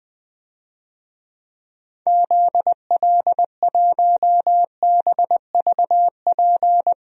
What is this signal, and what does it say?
Morse code, heard as a keyed tone, about 20 wpm, 700 Hz. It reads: ZL1BVP
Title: DXCC-100 Call Signs - 6 Characters - Repeated 2x - 20wpm